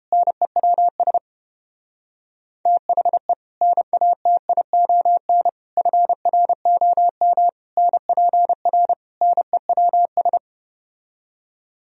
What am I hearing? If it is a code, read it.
Morse code, heard as a keyed tone, about 31 wpm, 700 Hz. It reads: NEWS THE NATION FROM NPR NEWS